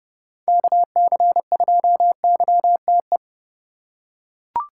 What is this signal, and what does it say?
Morse code, heard as a keyed tone, about 31 wpm, 700 Hz. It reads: KC2YTE
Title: USA Call Signs - 6 Characters - 4x Character Spacing - 30wpm